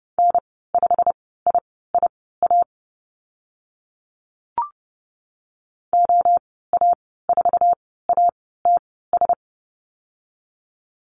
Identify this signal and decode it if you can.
Morse code, heard as a keyed tone, about 29 wpm, spaced as for 17 wpm, 700 Hz. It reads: N5IIA OA4ATS
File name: Lesson 15g - Number 5 - Call Signs 30wpm